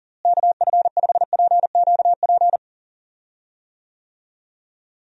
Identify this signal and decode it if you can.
Morse code, heard as a keyed tone, about 40 wpm, 700 Hz. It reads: KF5PQP
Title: USA Call Signs - 6 Characters - Repeated 2x - 40wpm